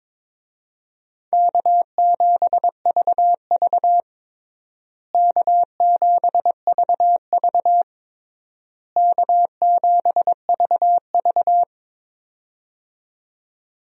Morse code, heard as a keyed tone, about 22 wpm, 700 Hz. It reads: K7VV K7VV K7VV
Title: USA Call Signs - 4 Characters - Repeated 3x - 22wpm